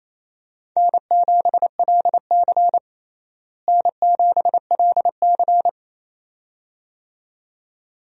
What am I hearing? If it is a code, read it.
Morse code, heard as a keyed tone, about 28 wpm, 700 Hz. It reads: N7LC N7LC
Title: USA Call Signs - 4 Characters - Repeated 2x - 28wpm